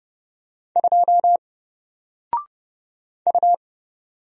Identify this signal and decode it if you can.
Morse code, heard as a keyed tone, about 30 wpm, 700 Hz. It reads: 2 U
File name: Single-Letter-Number-30wpm